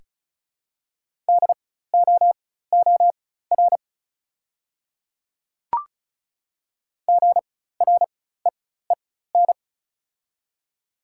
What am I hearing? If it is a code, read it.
Morse code, heard as a keyed tone, about 36 wpm, spaced as for 16 wpm, 690 Hz. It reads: DOOR GREEN
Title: 4x Character Spacing - Top 500 Words - 35wpm